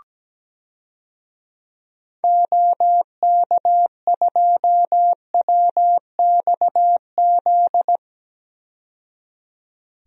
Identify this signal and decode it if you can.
Morse code, heard as a keyed tone, about 17 wpm, 700 Hz. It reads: OK2WXZ
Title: DXCC-100 Call Signs - 6 Characters - 17wpm